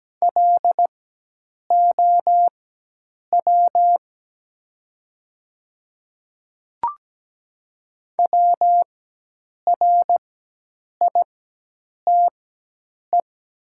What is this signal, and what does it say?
Morse code, heard as a keyed tone, about 17 wpm, spaced as for 8 wpm, 700 Hz. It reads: LOW WRITE
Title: 4x Character Spacing - Top 200 Words - 17wpm